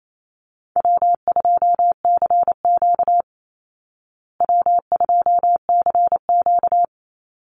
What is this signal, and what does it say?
Morse code, heard as a keyed tone, about 28 wpm, 700 Hz. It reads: W2CQ W2CQ
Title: Easy Call Signs Repeated 3x - 28wpm